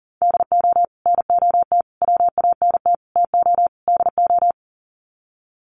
Morse code, heard as a keyed tone, about 40 wpm, 700 Hz. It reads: DO NOT WANT TO DO
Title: Sets of 5 Words 40wpm